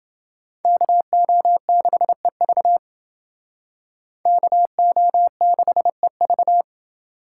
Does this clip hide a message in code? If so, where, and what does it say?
Yes: Morse code from 0.7 to 6.6 s, heard as a keyed tone, about 28 wpm, 700 Hz. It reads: KO6EV KO6EV